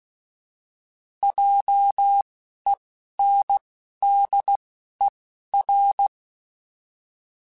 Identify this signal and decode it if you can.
Morse code, heard as a keyed tone, about 16 wpm, spaced as for 11 wpm, 800 Hz. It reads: JENDER